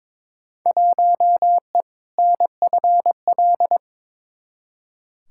Morse code, heard as a keyed tone, about 22 wpm, 700 Hz. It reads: 1E NFL
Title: Field Day Exchanges - 22wpm